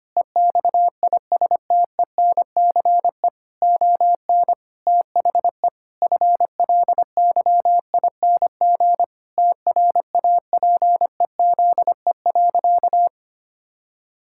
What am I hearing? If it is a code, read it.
Morse code, heard as a keyed tone, about 25 wpm, 700 Hz. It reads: EXISTENCE ON THE FLYING TRAPEZE.
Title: Pangram Challenge v2 - 25wpm